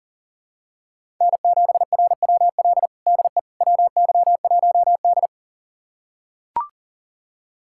Morse code, heard as a keyed tone, about 40 wpm, 700 Hz. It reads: N7RWL DE WY1D
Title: USA Callsign DE Callsign 40wpm